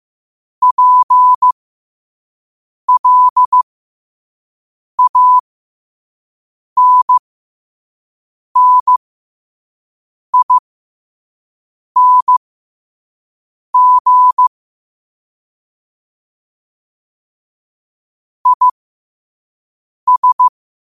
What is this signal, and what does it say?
Morse code, heard as a keyed tone, about 15 wpm, spaced as for 5 wpm, 1000 Hz. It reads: PLANNING IS
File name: qotdfiles / 5wpm